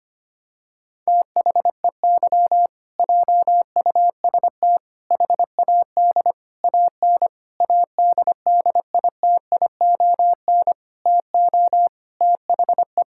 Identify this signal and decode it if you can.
Morse code, heard as a keyed tone, about 25 wpm, 700 Hz. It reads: THEY JUST HAD AN ADDITION TO THE
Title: Sentences from Top 1000 Words - Review Part 6 - 25wpm